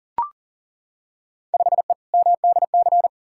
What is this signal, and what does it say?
Morse code, heard as a keyed tone, about 40 wpm, 700 Hz. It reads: 5E MDC
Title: Field Day Exchanges - 40wpm